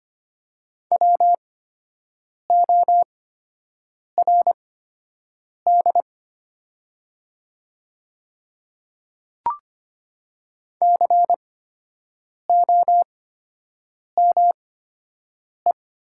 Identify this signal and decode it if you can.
Morse code, heard as a keyed tone, about 25 wpm, spaced as for 7 wpm, 700 Hz. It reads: WORD COME
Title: 8x Character Spacing - Top 100 Words - 25wpm